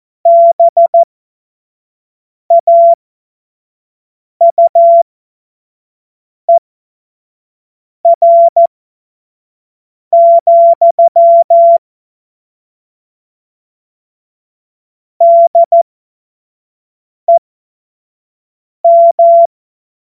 Morse code, heard as a keyed tone, about 14 wpm, spaced as for 5 wpm, 680 Hz. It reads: BAUER, DEM
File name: Morsekurs / Witze / Witz 13. April 2021, 25 F70